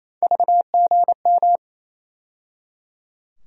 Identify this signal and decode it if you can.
Morse code, heard as a keyed tone, about 28 wpm, 700 Hz. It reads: VGM
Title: USA Call Sign Postfixes - 3 Letters 28wpm